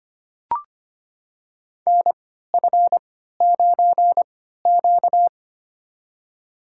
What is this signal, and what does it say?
Morse code, heard as a keyed tone, about 25 wpm, spaced as for 14 wpm, 700 Hz. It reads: NF9Q
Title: USA Call Signs - 4 Characters - 3x Character Spacing - 25wpm